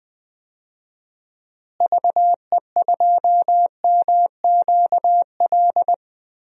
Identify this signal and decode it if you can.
Morse code, heard as a keyed tone, about 21 wpm, 700 Hz. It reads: VE2MQL
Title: DXCC-100 Call Signs - 6 Characters - Repeated 3x - 20wpm